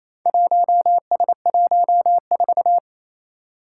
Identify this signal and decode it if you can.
Morse code, heard as a keyed tone, about 28 wpm, 700 Hz. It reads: 1S14
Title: Four Challenging Letter-Number Combinations 28wpm